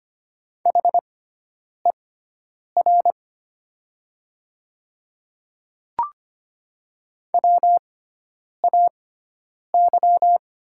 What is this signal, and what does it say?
Morse code, heard as a keyed tone, about 25 wpm, spaced as for 9 wpm, 700 Hz. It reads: HER WAY
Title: 6x Character Spacing - Top 300 Words - 25wpm